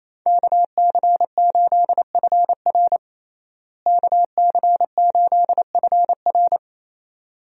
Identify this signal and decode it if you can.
Morse code, heard as a keyed tone, about 28 wpm, 700 Hz. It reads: KC8FR KC8FR
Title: USA Call Signs - 5 Characters - Repeated 2x - 28wpm